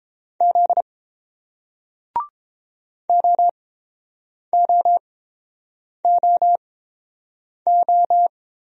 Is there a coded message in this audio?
Yes: Morse code from 0.4 to 8.3 s, heard as a keyed tone, about 30 wpm, spaced as for 7 wpm, 700 Hz. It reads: Z OOOO